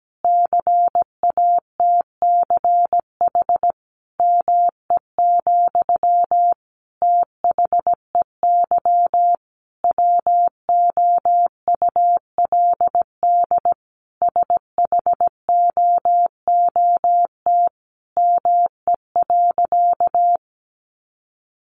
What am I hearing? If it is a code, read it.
Morse code, heard as a keyed tone, about 17 wpm, 700 Hz. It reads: CATCH ME, THEY WOULD SHOOT ME.
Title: Sentences from Top 1500 Words 17wpm